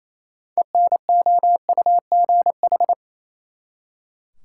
Morse code, heard as a keyed tone, about 28 wpm, 700 Hz. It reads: ENOUGH
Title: Six Letter Words - 28wpm